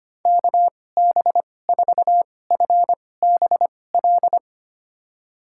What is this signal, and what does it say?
Morse code, heard as a keyed tone, about 25 wpm, spaced as for 18 wpm, 700 Hz. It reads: KB4FBL
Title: USA Call Signs - 6 Characters - 2x Character Spacing - 25wpm